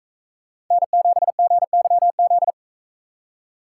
Morse code, heard as a keyed tone, about 40 wpm, 700 Hz. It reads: N7GYZ